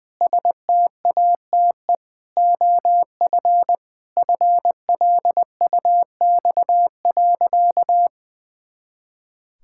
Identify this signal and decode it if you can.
Morse code, heard as a keyed tone, about 20 wpm, 700 Hz. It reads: STATE OF FLUX.
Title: Pangram Challenge v2 - 20wpm